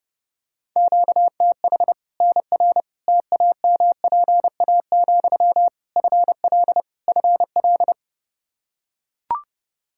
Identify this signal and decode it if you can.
Morse code, heard as a keyed tone, about 30 wpm, 700 Hz. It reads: QTH NR TAMPA, FL FL
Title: K1USN SST QSO Practice 30wpm